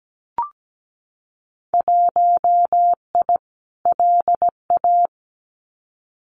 Morse code, heard as a keyed tone, about 17 wpm, 700 Hz. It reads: 1I LA
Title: Winter Field Day Exchanges Encore 17wpm